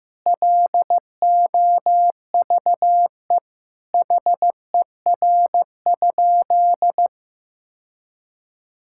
Morse code, heard as a keyed tone, about 15 wpm, 700 Hz. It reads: LOVE HER?
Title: Sentences from Top 1600 Words 15wpm